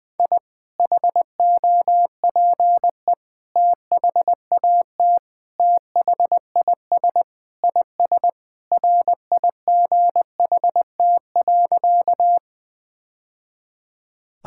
Morse code, heard as a keyed tone, about 20 wpm, 700 Hz. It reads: I HOPE THAT THIS IS RIGHT.
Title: Sentences with 3-Word Phrases - 20wpm